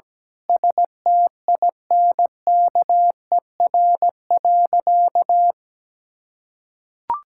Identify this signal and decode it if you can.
Morse code, heard as a keyed tone, about 17 wpm, 700 Hz. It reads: STINKER.
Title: Pangram Challenge v2 - 17wpm